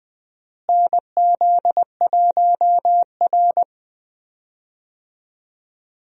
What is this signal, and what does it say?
Morse code, heard as a keyed tone, about 20 wpm, 700 Hz. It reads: NZ1R
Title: USA Call Signs - 4 Characters - Repeated 3x - 20wpm